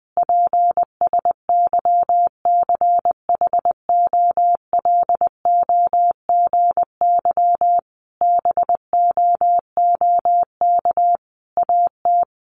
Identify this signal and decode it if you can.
Morse code, heard as a keyed tone, about 20 wpm, 700 Hz. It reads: PSYCHOLOGY BOOK AT
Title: Sentences from Top 1400 Words 20wpm